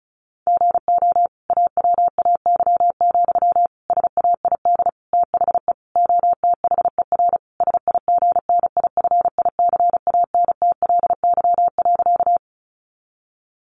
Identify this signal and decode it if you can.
Morse code, heard as a keyed tone, about 34 wpm, 700 Hz. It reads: GO AWAY, SAID THE OTHER SIGNIFICANTLY.
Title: Sentences from Top 1500 Words 35wpm